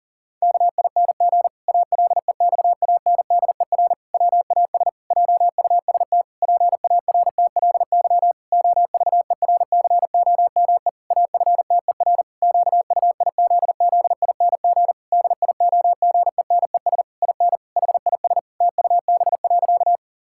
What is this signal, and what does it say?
Morse code, heard as a keyed tone, about 37 wpm, 700 Hz. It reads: KING ALEXANDER WAS JUST PARTLY OVERCOME AFTER QUIZZING DIOGENES IN HIS TUB.
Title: Pangram Challenge v2 - 40wpm